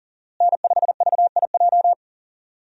Morse code, heard as a keyed tone, about 39 wpm, 700 Hz. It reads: N5VIJ
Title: USA Call Signs - 5 Characters - Repeated 2x - 40wpm